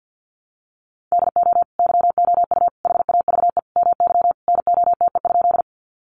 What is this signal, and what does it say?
Morse code, heard as a keyed tone, about 50 wpm, 700 Hz. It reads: DO YOU HAVE MY NOTE?
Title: Sentences from Top 500 Words 50wpm